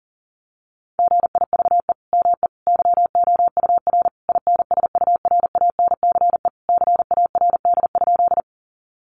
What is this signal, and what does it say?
Morse code, heard as a keyed tone, about 40 wpm, 700 Hz. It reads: GIVE ME YOUR INSURANCE CARD?
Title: Sentences from Top 1100 Words 40wpm